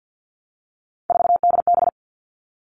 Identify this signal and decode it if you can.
Morse code, heard as a keyed tone, about 50 wpm, 700 Hz. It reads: 4DB
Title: All Cut Numbers - 3-Digit 50wpm